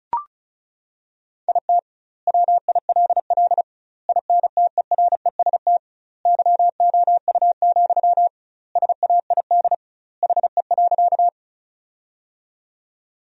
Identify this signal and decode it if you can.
Morse code, heard as a keyed tone, about 34 wpm, 700 Hz. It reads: IT WILL INTEREST YOU, SAID HE.
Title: 2x Word-Spacing - Sentences from Top 400 Words - 35wpm